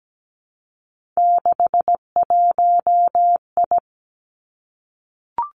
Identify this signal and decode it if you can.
Morse code, heard as a keyed tone, about 17 wpm, 700 Hz. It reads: 61I
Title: Three Challenging Letter-Number Combinations 17wpm